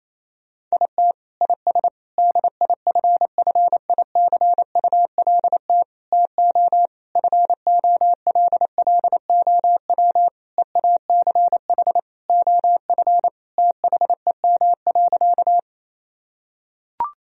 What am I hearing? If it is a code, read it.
Morse code, heard as a keyed tone, about 28 wpm, 700 Hz. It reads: IT IS DIFFICULT TO FOLLOW EACH OF THEM.
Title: Sentences from Top 1000 Words - Review Part 1 - 28wpm